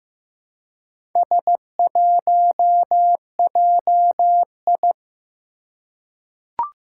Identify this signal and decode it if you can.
Morse code, heard as a keyed tone, about 15 wpm, 700 Hz. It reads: S1JI
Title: Four Challenging Letter-Number Combinations 15wpm